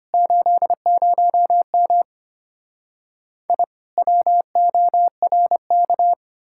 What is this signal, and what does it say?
Morse code, heard as a keyed tone, about 32 wpm, 700 Hz. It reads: 80M I WORK